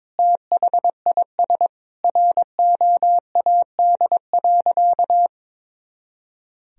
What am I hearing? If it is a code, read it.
Morse code, heard as a keyed tone, about 21 wpm, 700 Hz. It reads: THIS ROAD.
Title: Sentences from Top 400 Words and Abbreviations 22wpm